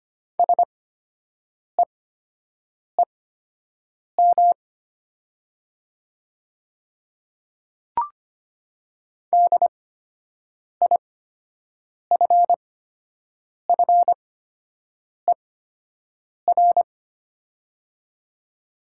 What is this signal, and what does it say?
Morse code, heard as a keyed tone, about 25 wpm, spaced as for 7 wpm, 690 Hz. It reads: SEEM DIFFER
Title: 8x Character Spacing - Top 500 Words - 25wpm